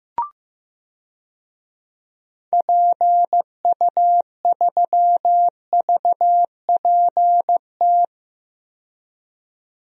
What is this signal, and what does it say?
Morse code, heard as a keyed tone, about 15 wpm, 700 Hz. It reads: PU3VPT
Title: DXCC-100 Call Signs - 6 Characters - 15wpm